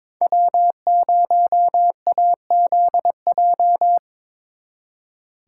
Morse code, heard as a keyed tone, about 26 wpm, 700 Hz. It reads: W0AZJ
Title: Speed Racing - USA 1x3 Call Signs 22wpm